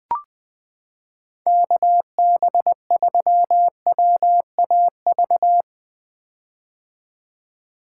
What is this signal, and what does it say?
Morse code, heard as a keyed tone, about 20 wpm, 700 Hz. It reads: KB3WAV
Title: 2023 POTA Activator Call Signs - 20wpm